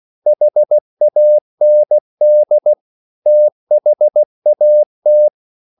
Morse code, heard as a keyed tone, about 16 wpm, 600 Hz. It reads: HAND THAT
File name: CW/Practice